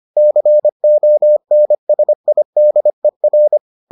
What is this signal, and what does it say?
Morse code, heard as a keyed tone, about 25 wpm, 600 Hz. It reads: CONSIDER